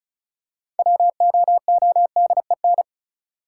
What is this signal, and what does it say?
Morse code, heard as a keyed tone, about 36 wpm, 700 Hz. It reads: WOODEN
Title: Six Letter Words - 35wpm